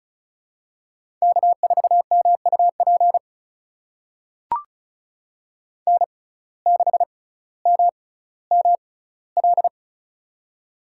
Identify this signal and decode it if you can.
Morse code, heard as a keyed tone, about 36 wpm, spaced as for 18 wpm, 700 Hz. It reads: K4MUP N6MML
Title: USA Call Signs - 5 Characters - 6x Character Spacing - 35wpm